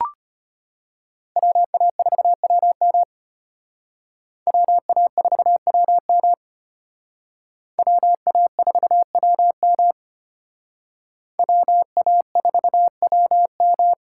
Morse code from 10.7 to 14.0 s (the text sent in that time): WA4WM